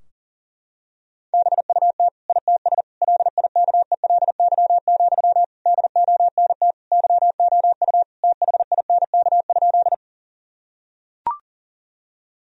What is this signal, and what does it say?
Morse code, heard as a keyed tone, about 40 wpm, 700 Hz. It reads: BUT ITS LIKELY, DONT YOU THINK?
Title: Sentences from Top 1000 Words - Review Part 4 - 40wpm